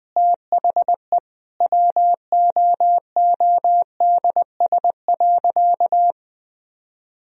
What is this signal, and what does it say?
Morse code, heard as a keyed tone, about 20 wpm, 700 Hz. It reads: THE WOODS.
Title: Sentences from Top 1000 Words - Review Part 2 - 20wpm